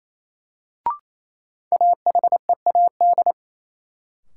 Morse code, heard as a keyed tone, about 28 wpm, 700 Hz. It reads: AHEAD
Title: Sentences from Top 1000 Words - Review Part 6 - New Word Drill - 28wpm